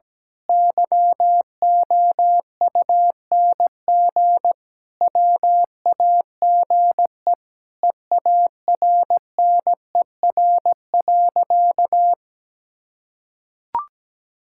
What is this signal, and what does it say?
Morse code, heard as a keyed tone, about 17 wpm, 700 Hz. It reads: YOUNG WAGE EARNER.
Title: Pangram Challenge v2 - 17wpm